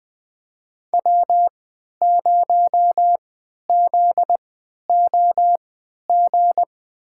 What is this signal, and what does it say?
Morse code, heard as a keyed tone, about 20 wpm, spaced as for 11 wpm, 700 Hz. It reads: W0ZOG
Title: USA Call Signs - 5 Characters - 3x Character Spacing - 20wpm